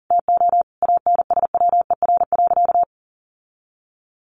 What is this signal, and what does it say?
Morse code, heard as a keyed tone, about 40 wpm, 700 Hz. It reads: TO ANSWER.
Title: Sentences from Top 1500 Words 40wpm